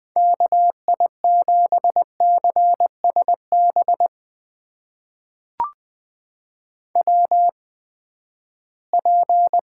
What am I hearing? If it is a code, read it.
Morse code, heard as a keyed tone, about 20 wpm, spaced as for 14 wpm, 700 Hz. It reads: KI7CSB WP
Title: USA Call Signs - 6 Characters - 8x Character Spacing - 20wpm